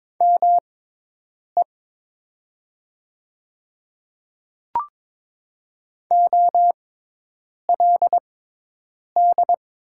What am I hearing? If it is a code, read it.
Morse code, heard as a keyed tone, about 22 wpm, spaced as for 8 wpm, 700 Hz. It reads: ME OLD
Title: 6x Character Spacing - Top 200 Words - 22wpm